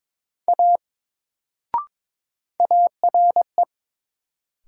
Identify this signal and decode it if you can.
Morse code, heard as a keyed tone, about 22 wpm, 700 Hz. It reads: A ARE
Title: Sentences from Top 100 Words - New Word Drill - 22wpm